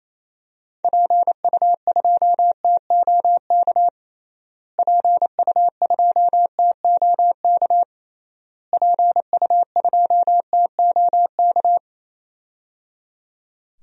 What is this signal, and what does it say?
Morse code, heard as a keyed tone, about 27 wpm, 700 Hz. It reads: PU2TOK PU2TOK PU2TOK
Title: DXCC-100 Call Signs - 6 Characters - Repeated 3x - 28wpm